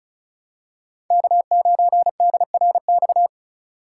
Morse code, heard as a keyed tone, about 36 wpm, 700 Hz. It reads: K9DRX
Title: USA Call Signs - 5 Characters - 4x Character Spacing - 35wpm